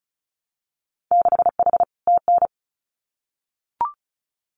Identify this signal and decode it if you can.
Morse code, heard as a keyed tone, about 34 wpm, 700 Hz. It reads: 6H TN
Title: Winter Field Day Exchanges Encore 35wpm